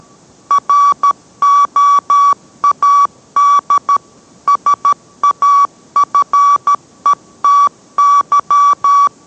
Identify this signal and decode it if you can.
Morse code, heard as a keyed tone, about 13 wpm, 1140 Hz. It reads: ROAD SAFETY